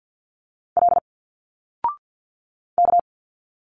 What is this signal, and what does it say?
Morse code, heard as a keyed tone, about 50 wpm, 700 Hz. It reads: L K
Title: Single Letters 50wpm